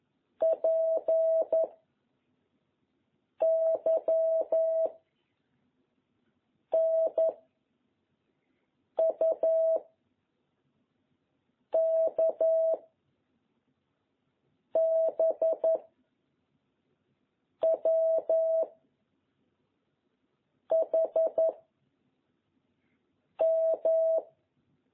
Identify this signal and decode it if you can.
Morse code, heard as a keyed tone, about 11 wpm, 660 Hz. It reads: P Y N U K B W H M
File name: jh8cbh / 2021cw